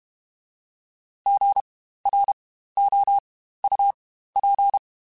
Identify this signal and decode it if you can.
Morse code, heard as a keyed tone, about 32 wpm, spaced as for 15 wpm, 800 Hz. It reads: GROUP
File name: Spaced Alphas / Spaced Alphas mp3s / Gs sp @30